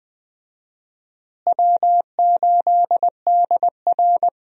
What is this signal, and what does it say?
Morse code, heard as a keyed tone, about 20 wpm, 700 Hz. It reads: W8DR
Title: USA Call Signs - 4 Characters - Repeated 2x - 20wpm